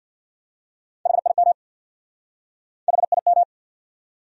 Morse code, heard as a keyed tone, about 59 wpm, 700 Hz. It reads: HIM HIM
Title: Speed-Racing Top 300 Words 50wpm